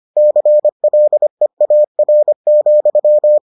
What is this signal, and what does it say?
Morse code, heard as a keyed tone, about 25 wpm, 600 Hz. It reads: CLEAR,